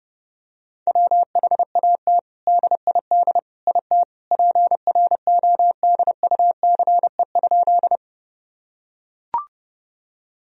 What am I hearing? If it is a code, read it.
Morse code, heard as a keyed tone, about 30 wpm, 700 Hz. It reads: WHAT DID IT PRODUCE?
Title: Sentences from Top 1000 Words - Review Part 3 - 30wpm